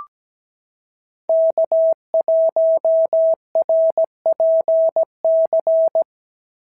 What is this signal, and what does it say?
Morse code, heard as a keyed tone, about 16 wpm, 650 Hz. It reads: K1RPC
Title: SOTA QSO Practice 17wpm